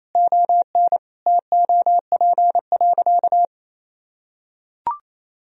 Morse code, heard as a keyed tone, about 28 wpm, 700 Hz. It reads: ON TOP.
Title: Sentences from Top 1000 Words - Review Part 2 - 28wpm